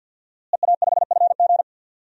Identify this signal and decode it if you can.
Morse code, heard as a keyed tone, about 52 wpm, 700 Hz. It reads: EA5FG
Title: DXCC-100 Call Signs - 5 Characters - Repeated 2x - 50wpm